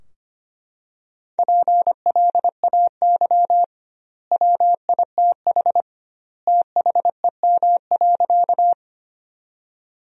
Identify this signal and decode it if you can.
Morse code, heard as a keyed tone, about 25 wpm, 700 Hz. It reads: PLAY WITH THEM.
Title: 2x Word-Spacing - Sentences from Top 200 Words - 25wpm